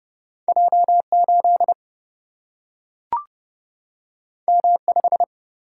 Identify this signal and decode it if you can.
Morse code, heard as a keyed tone, about 31 wpm, 700 Hz. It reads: J 8 M 5
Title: Single-Letter Single-Number 30wpm